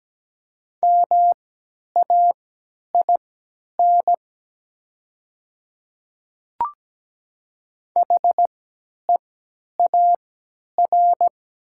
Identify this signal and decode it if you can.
Morse code, heard as a keyed tone, about 17 wpm, spaced as for 10 wpm, 700 Hz. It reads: MAIN HEAR
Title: 3x Character Spacing - Top 300 Words - 17wpm